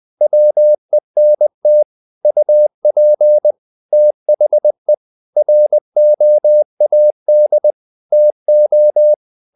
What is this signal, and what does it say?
Morse code, heard as a keyed tone, about 20 wpm, 600 Hz. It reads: WENT UP THE ROAD TO
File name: the old man and the sea / the old man and the sea part 1 20wpm